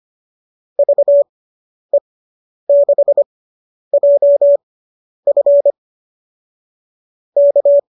Morse code, heard as a keyed tone, about 25 wpm, spaced as for 10 wpm, 570 Hz. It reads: VE6JF K